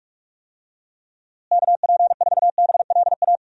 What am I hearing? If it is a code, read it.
Morse code, heard as a keyed tone, about 41 wpm, 700 Hz. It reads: KP4BLA
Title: DXCC-100 Call Signs - 6 Characters - Repeated 3x - 45wpm